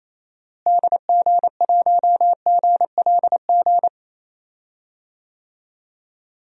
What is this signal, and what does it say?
Morse code, heard as a keyed tone, about 28 wpm, 700 Hz. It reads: DG1GLG
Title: DXCC-100 Call Signs - 6 Characters - Repeated 2x - 28wpm